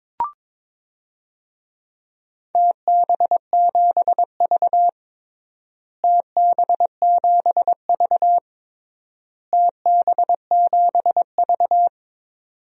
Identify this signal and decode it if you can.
Morse code, heard as a keyed tone, about 22 wpm, 700 Hz. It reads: TB7V TB7V TB7V
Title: DXCC-100 Call Signs - 4 Characters - Repeated 3x - 22wpm